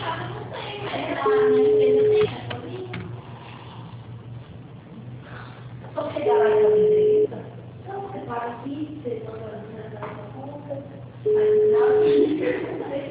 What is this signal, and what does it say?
Ringback tone (Europe, 425 Hz), with speech.